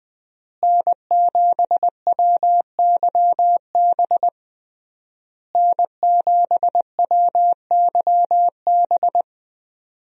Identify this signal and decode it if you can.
Morse code, heard as a keyed tone, about 20 wpm, 700 Hz. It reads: N7WYB N7WYB
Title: USA Call Signs - 5 Characters - Repeated 3x - 20wpm